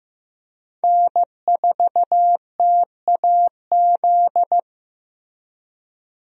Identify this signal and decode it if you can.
Morse code, heard as a keyed tone, about 15 wpm, 700 Hz. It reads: N4TAZ
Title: USA Call Signs - 5 Characters - Repeated 2x - 15wpm